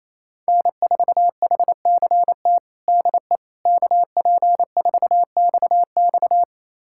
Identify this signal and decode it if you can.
Morse code, heard as a keyed tone, about 28 wpm, 700 Hz. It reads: N4HCT DE KP4XX
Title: USA Callsign DE Callsign - 5 Characters - 28wpm